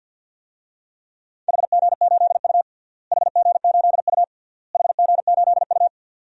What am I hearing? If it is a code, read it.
Morse code, heard as a keyed tone, about 53 wpm, 700 Hz. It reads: HG8U HG8U HG8U
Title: DXCC-100 Call Signs - 4 Characters - Repeated 3x - 50wpm